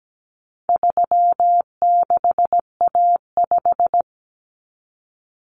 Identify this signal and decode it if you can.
Morse code, heard as a keyed tone, about 17 wpm, 700 Hz. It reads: 36A5
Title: Common Cut Numbers - 4-Digit 17wpm